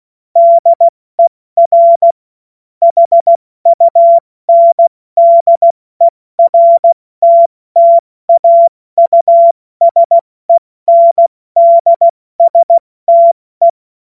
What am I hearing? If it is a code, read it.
Morse code, heard as a keyed tone, about 16 wpm, 680 Hz. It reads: DER HUNDERTTAUSENDSTE
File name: Morsekurs / Witze / Witz 28. Dezember 2021, 70 F80